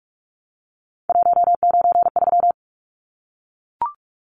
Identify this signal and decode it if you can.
Morse code, heard as a keyed tone, about 44 wpm, 700 Hz. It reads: 193
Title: Numbers - 3-Digit 45wpm